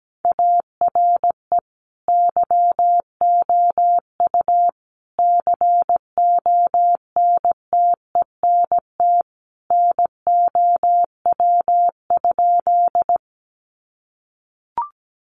Morse code, heard as a keyed tone, about 17 wpm, 700 Hz. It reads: ARE YOU CONTENT NOW?
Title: Sentences from Top 900 Words 17wpm